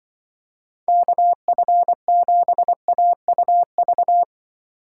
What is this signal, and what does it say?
Morse code, heard as a keyed tone, about 24 wpm, 700 Hz. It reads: KF7AUV
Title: Speed Racing - USA 2x3 Call Signs 20wpm